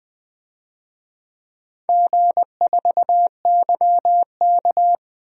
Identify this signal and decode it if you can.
Morse code, heard as a keyed tone, about 20 wpm, 700 Hz. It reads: G4YK
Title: DXCC-100 Call Signs - 4 Characters - 20wpm